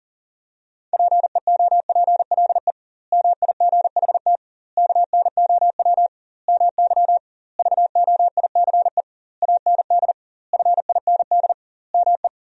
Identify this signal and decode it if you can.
Morse code, heard as a keyed tone, about 40 wpm, 700 Hz. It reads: PEOPLE MIGHT KNOW MY VOICE AND FIND ME
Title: 2x Word-Spacing - Sentences from Top 500 Words - 40wpm